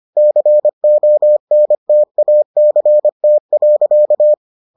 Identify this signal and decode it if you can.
Morse code, heard as a keyed tone, about 25 wpm, 600 Hz. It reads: CONTACT.